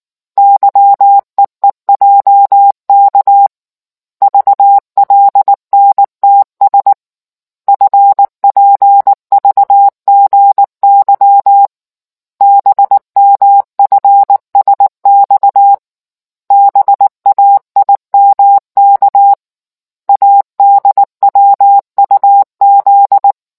Morse code, heard as a keyed tone, about 19 wpm, 800 Hz. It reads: YEEJK VLNTS FPVGY BMFSX BAIMK ADWUZ